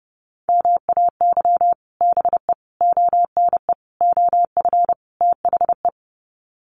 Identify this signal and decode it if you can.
Morse code, heard as a keyed tone, about 30 wpm, 700 Hz. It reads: MAY BE ONE OF THE
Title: Sets of 5 Words Encore2 30wpm